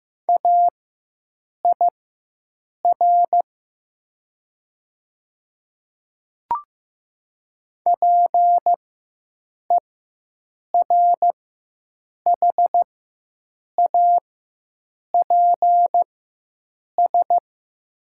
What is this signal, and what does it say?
Morse code, heard as a keyed tone, about 15 wpm, spaced as for 7 wpm, 700 Hz. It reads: AIR PERHAPS
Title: 4x Character Spacing - Top 500 Words - 15wpm